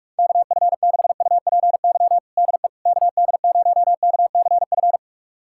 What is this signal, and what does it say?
Morse code, heard as a keyed tone, about 42 wpm, 700 Hz. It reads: KF6UPY DE KD0KCF
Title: USA Callsign DE Callsign - 6 Characters - 45wpm